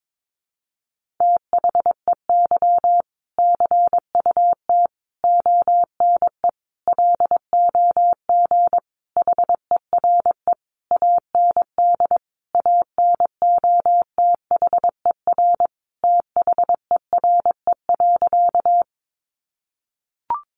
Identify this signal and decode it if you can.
Morse code, heard as a keyed tone, about 22 wpm, 700 Hz. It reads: THEY CUT ONE LOG HERE AND ANOTHER THERE.
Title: Sentences from Top 1400 Words 22wpm